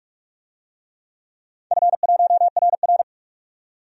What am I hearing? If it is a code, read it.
Morse code, heard as a keyed tone, about 44 wpm, 700 Hz. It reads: F1RR
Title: DXCC-100 Call Signs - 4 Characters - 45wpm